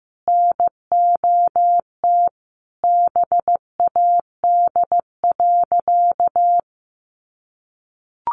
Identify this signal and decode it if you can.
Morse code, heard as a keyed tone, about 15 wpm, 700 Hz. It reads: NOT BAD.
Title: Sentences from Top 900 Words 15wpm